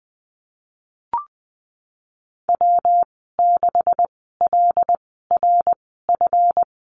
Morse code, heard as a keyed tone, about 20 wpm, spaced as for 15 wpm, 700 Hz. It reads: W6LRF
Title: 140 Lesson 23g - Letter f - Call Signs 20wpm